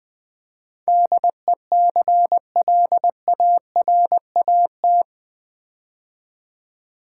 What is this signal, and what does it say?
Morse code, heard as a keyed tone, about 20 wpm, 700 Hz. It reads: DECLARAT
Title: Build A Word - 4 Letters At A Time - 11-Letter Words - 20wpm